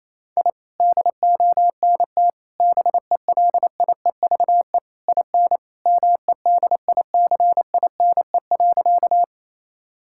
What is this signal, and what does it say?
Morse code, heard as a keyed tone, about 28 wpm, 700 Hz. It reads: I DONT BELIEVE IN MEDICINE.
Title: Sentences from Top 1000 Words - Review Part 6 - 28wpm